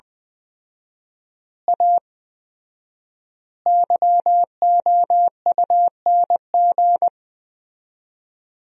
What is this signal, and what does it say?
Morse code, heard as a keyed tone, about 21 wpm, 700 Hz. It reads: A YOUNG
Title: 4x Word-Spacing - Sentences from Top 500 Words - 20wpm